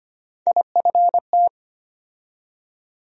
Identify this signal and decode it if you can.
Morse code, heard as a keyed tone, about 25 wpm, 700 Hz. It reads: IFT
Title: USA Call Sign Postfixes - 3 Letters 25wpm